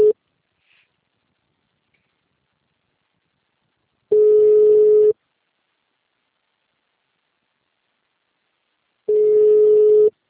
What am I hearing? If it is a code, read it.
Ringback tone (Europe, 425 Hz).
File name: out- -2002- - - .35942